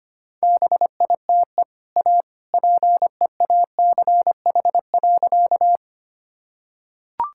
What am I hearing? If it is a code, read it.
Morse code, heard as a keyed tone, about 25 wpm, 700 Hz. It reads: BITE A PEACH.
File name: Easy Sentences with Hints - 25wpm